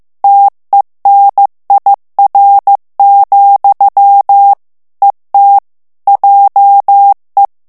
Morse code, heard as a keyed tone, about 15 wpm, 800 Hz. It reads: TENIR, ET JE
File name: morse mp3 / FRANCAIS TEXTE CLAIR 2 VITESSE 14 MOTS MINUTES